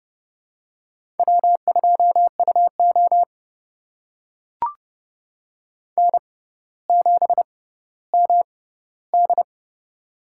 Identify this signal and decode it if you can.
Morse code, heard as a keyed tone, about 29 wpm, spaced as for 15 wpm, 700 Hz. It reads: W2UO N7MD
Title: USA Call Signs - 4 Characters - 6x Character Spacing - 30wpm